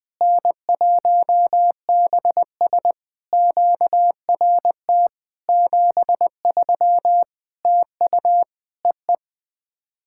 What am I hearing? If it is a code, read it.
Morse code, heard as a keyed tone, about 20 wpm, 700 Hz. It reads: N1BS QRT 73 TU EE
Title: POTA QSO Practice - 20wpm